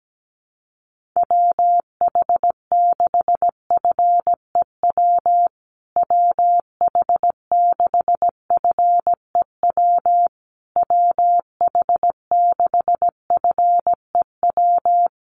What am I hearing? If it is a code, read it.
Morse code, heard as a keyed tone, about 17 wpm, 700 Hz. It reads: WH6FEW WH6FEW WH6FEW
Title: USA Call Signs Repeated 3x 17wpm